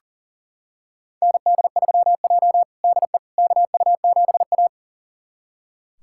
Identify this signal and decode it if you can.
Morse code, heard as a keyed tone, about 38 wpm, 700 Hz. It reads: ND3J DE KU7A
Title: USA Callsign DE Callsign - 4 Characters - 40wpm